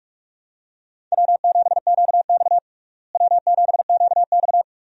Morse code, heard as a keyed tone, about 45 wpm, 700 Hz. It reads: W7QX W7QX
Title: USA Call Signs - 4 Characters - Repeated 2x - 45wpm